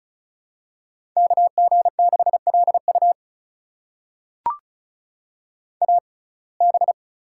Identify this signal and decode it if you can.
Morse code, heard as a keyed tone, about 36 wpm, spaced as for 25 wpm, 700 Hz. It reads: KG6LU AB
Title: USA Call Signs - 5 Characters - 6x Character Spacing - 35wpm